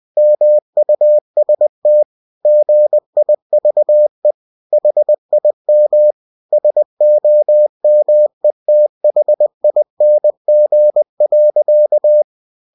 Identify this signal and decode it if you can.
Morse code, heard as a keyed tone, about 20 wpm, 600 Hz. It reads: MUST GIVE HIM SOMETHING.
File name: cw-books / winnie-the-pooh / winnie-20wpm-page-0050